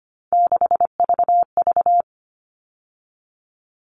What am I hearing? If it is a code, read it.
Morse code, heard as a keyed tone, about 25 wpm, 700 Hz. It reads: 6VV
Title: All Cut Numbers - 3-Digit 25wpm